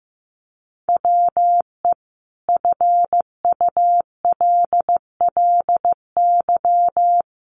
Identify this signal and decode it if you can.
Morse code, heard as a keyed tone, about 15 wpm, 700 Hz. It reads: WE FULLY
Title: Sentences from Top 1300 Words 15wpm